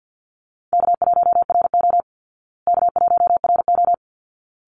Morse code, heard as a keyed tone, about 50 wpm, 700 Hz. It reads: K1RO K1RO
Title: Easy Call Signs Repeated 2x - 50wpm